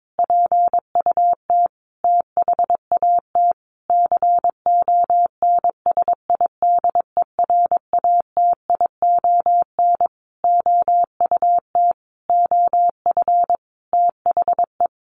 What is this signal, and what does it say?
Morse code, heard as a keyed tone, about 22 wpm, 700 Hz. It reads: PUT THAT CONSIDERATION OUT OF THE
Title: Sentences from Top 1500 Words 22wpm